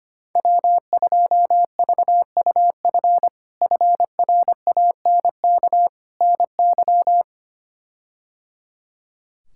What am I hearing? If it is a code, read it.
Morse code, heard as a keyed tone, about 25 wpm, 700 Hz. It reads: W2VUF FRANK NY
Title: K1USN SST QSO Practice - 25wpm